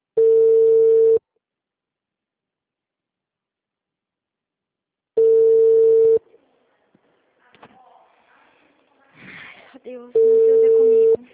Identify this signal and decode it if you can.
Ringback tone (China, 450 Hz), with speech.